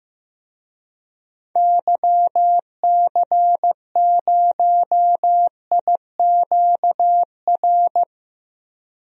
Morse code, heard as a keyed tone, about 15 wpm, 700 Hz. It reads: YC0IQR
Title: DXCC-100 Call Signs - 6 Characters - Repeated 2x - 15wpm